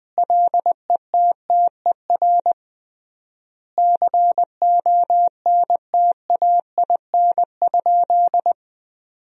Morse code, heard as a keyed tone, about 20 wpm, 700 Hz. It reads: LETTER CONTAIN?
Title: 3x Word-Spacing - Sentences from Top 500 Words - 20wpm